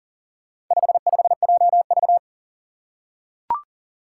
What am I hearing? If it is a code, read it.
Morse code, heard as a keyed tone, about 40 wpm, 700 Hz. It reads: 55JV
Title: Four Challenging Letter-Number Combinations 40wpm